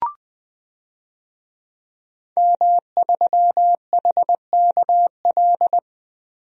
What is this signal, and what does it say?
Morse code, heard as a keyed tone, about 20 wpm, 700 Hz. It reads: M3HKL
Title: DXCC-100 Call Signs - 5 Characters - Repeated 2x - 20wpm